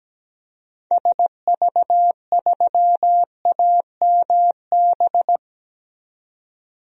Morse code, heard as a keyed tone, about 17 wpm, 700 Hz. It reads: SV3AMB
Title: DXCC-100 Call Signs - 6 Characters - 17wpm